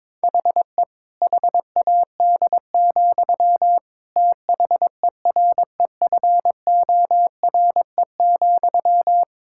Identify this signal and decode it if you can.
Morse code, heard as a keyed tone, about 22 wpm, 700 Hz. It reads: HE HAD, THEREFORE,
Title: Sentences from Top 1000 Words - Review Part 5 - 22wpm